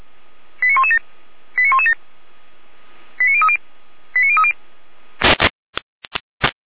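Five-tone selective call (ZVEI), heard as quick runs of five tones, tones of about 70 ms. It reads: 89117 89117 89030 89030
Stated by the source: ZVEI five-tone selective call 89080, received via SDR.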